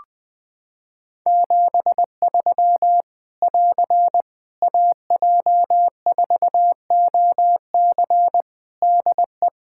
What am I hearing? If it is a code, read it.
Morse code, heard as a keyed tone, about 21 wpm, 700 Hz. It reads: 73 <AR> AJ4OC DE
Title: Practice QSO - Exchange 4 - Ending - 20wpm